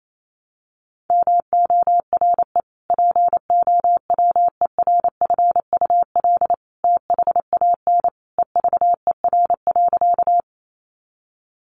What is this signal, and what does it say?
Morse code, heard as a keyed tone, about 28 wpm, 700 Hz. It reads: MORE POWERFUL THAN EVER.
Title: Sentences from Top 1300 Words 28wpm